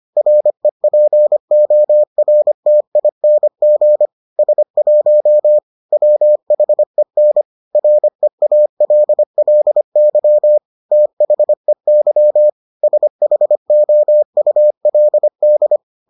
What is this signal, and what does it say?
Morse code, heard as a keyed tone, about 25 wpm, 600 Hz. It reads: REPORTING S1 WHEN REALLY THEY SHOULD